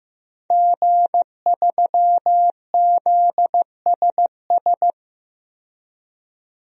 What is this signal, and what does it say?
Morse code, heard as a keyed tone, about 15 wpm, 700 Hz. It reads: G3ZSS
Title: DXCC-100 Call Signs - 5 Characters - 15wpm